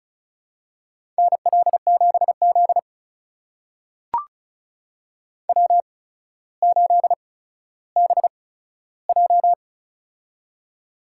Morse code, heard as a keyed tone, about 36 wpm, spaced as for 15 wpm, 700 Hz. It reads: NL7Z W8BJ
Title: USA Call Signs - 4 Characters - 8x Character Spacing - 35wpm